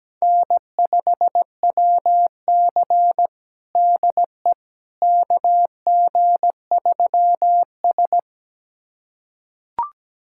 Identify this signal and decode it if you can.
Morse code, heard as a keyed tone, about 17 wpm, 700 Hz. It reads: N5WC DE KG3S
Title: USA Callsign DE Callsign - 4 Characters - 17wpm